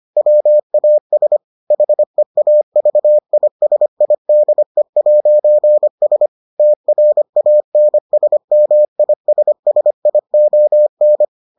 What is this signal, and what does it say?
Morse code, heard as a keyed tone, about 25 wpm, 600 Hz. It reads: WAS HEAVISIDE'S TRANSMISSION